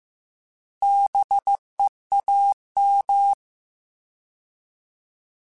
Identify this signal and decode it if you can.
Morse code, heard as a keyed tone, about 15 wpm, 800 Hz. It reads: BEAM
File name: Alphabetical / Alphabetical mp3s / Bs @13